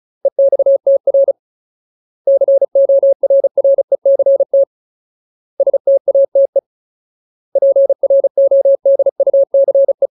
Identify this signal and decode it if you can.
Morse code, heard as a keyed tone, about 34 wpm, 550 Hz. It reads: EXTR CORRECT STATE PRODUCE
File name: Ultimate Word List / 35WPM Ultimate